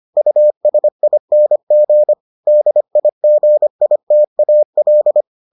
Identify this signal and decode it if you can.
Morse code, heard as a keyed tone, about 25 wpm, 600 Hz. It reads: USING DIGITAL